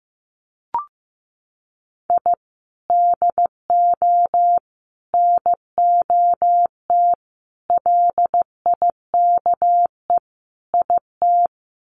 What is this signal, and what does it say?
Morse code, heard as a keyed tone, about 15 wpm, 700 Hz. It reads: I DO NOT LIKE IT
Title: Sets of 5 Words Encore 15wpm